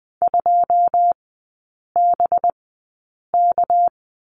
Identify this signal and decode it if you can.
Morse code, heard as a keyed tone, about 20 wpm, spaced as for 8 wpm, 690 Hz. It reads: 2BK
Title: 212 Lesson 34e - Number 8 - Three Characters 20wpm